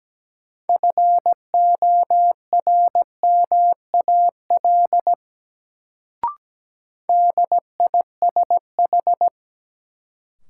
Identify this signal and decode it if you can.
Morse code, heard as a keyed tone, about 17 wpm, 700 Hz. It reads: FORMAL DISH
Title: Sentences from Top 1300 Words - New Word Drill Alt - 17wpm